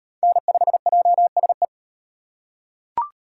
Morse code, heard as a keyed tone, about 38 wpm, 700 Hz. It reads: N5JSE